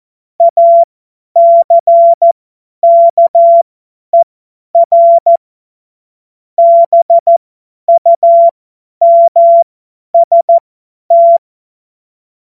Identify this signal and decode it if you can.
Morse code, heard as a keyed tone, about 14 wpm, spaced as for 10 wpm, 680 Hz. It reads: ACKER BUMST
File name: Witz-09-September-2025-50-F70